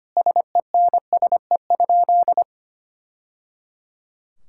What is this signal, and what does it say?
Morse code, heard as a keyed tone, about 25 wpm, 700 Hz. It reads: SENSE?
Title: Sentences from Top 1000 Words - Review Part 5 - 25wpm